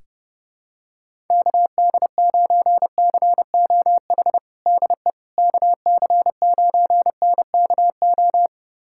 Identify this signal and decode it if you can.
Morse code, heard as a keyed tone, about 30 wpm, 700 Hz. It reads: KD9COH DE KC9NKO
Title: USA Callsign DE Callsign - 6 Characters - 30wpm